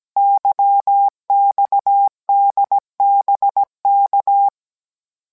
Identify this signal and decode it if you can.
Morse code, heard as a keyed tone, about 17 wpm, 800 Hz. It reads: YXDBK